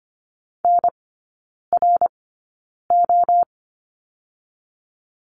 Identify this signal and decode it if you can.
Morse code, heard as a keyed tone, about 25 wpm, spaced as for 9 wpm, 700 Hz. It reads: NRO
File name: course / 25:10wpm / 093 Lesson 16e - Letter c - Three Characters 25wpm